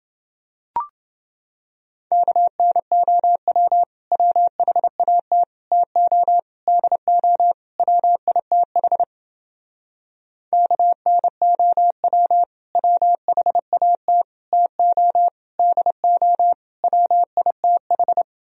Morse code, heard as a keyed tone, about 30 wpm, 700 Hz. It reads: KNOW WHAT TO DO WITH KNOW WHAT TO DO WITH